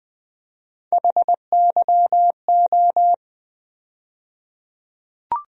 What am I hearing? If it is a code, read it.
Morse code, heard as a keyed tone, about 20 wpm, 700 Hz. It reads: HYO
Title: USA Call Sign Postfixes - 3 Letters 20wpm